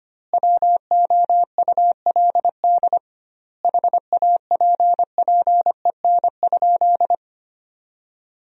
Morse code, heard as a keyed tone, about 25 wpm, 700 Hz. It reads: WOULD HAPPEN?
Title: 2x Word-Spacing - Sentences from Top 400 Words - 25wpm